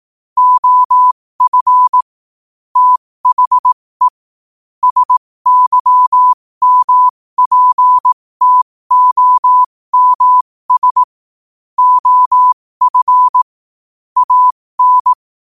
Morse code, heard as a keyed tone, about 18 wpm, spaced as for 15 wpm, 1000 Hz. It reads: OF THE SYMPTOMS OF AN
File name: qotdfiles / 15wpm